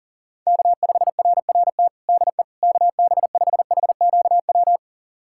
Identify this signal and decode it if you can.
Morse code, heard as a keyed tone, about 40 wpm, 700 Hz. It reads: K5RRT DE KB5HQW
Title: USA Callsign DE Callsign 40wpm